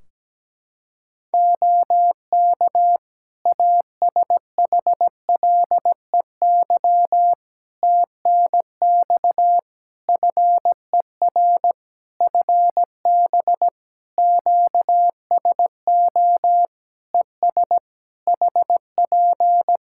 Morse code, heard as a keyed tone, about 17 wpm, 700 Hz. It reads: OK ASHLEY TNX FER FB QSO ES HP
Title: Practice QSO - Exchange 4 - Ending - 17wpm